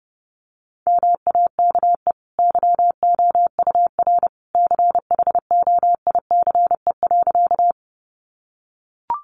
Morse code, heard as a keyed tone, about 30 wpm, 700 Hz. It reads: MAKE YOUR CHOICE.
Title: Sentences from Top 900 Words 30wpm